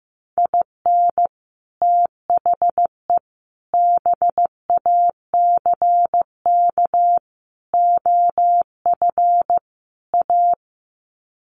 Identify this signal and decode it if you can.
Morse code, heard as a keyed tone, about 15 wpm, 700 Hz. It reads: IN THE BACK OF A
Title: Sets of 5 Words Encore2 15wpm